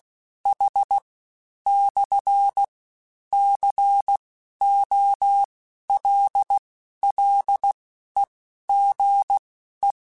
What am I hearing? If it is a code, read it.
Morse code, heard as a keyed tone, about 16 wpm, spaced as for 11 wpm, 800 Hz. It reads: H/COLLEGE
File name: QSOs / QSOs-mp3s / WD0BGP 1@15